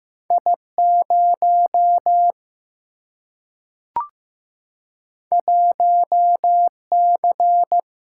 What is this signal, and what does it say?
Morse code, heard as a keyed tone, about 15 wpm, 700 Hz. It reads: I 0 1 C
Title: Single-Letter Single-Number 15wpm